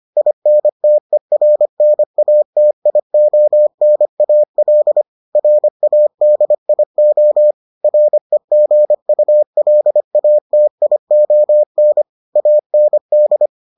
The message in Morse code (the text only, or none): INTERNATIONAL RADIO REGULATION AND